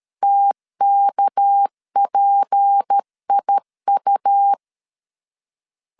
Morse code, heard as a keyed tone, about 13 wpm, 800 Hz. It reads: TKPIU
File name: CW / DET / 1 6 1500C S B 12wpm 3431